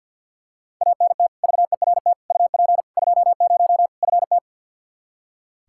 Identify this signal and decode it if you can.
Morse code, heard as a keyed tone, about 46 wpm, 700 Hz. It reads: ANT VERT UP 20 FT
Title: Practice QSO - Exchange 2 - Info - 50wpm